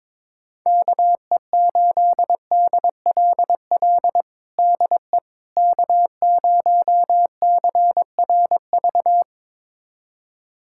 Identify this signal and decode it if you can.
Morse code, heard as a keyed tone, about 22 wpm, 700 Hz. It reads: KE8DLL DE K0CRV
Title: USA Callsign DE Callsign 22wpm